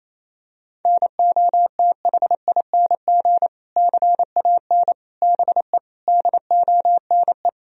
Morse code, heard as a keyed tone, about 28 wpm, 700 Hz. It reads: NOTHING CAN BE DONE
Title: Sentences from Top 1000 Words - Review Part 2 - 28wpm